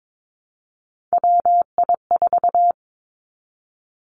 Morse code, heard as a keyed tone, about 22 wpm, 700 Hz. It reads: WI4
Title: Three Challenging Letter-Number Combinations 22wpm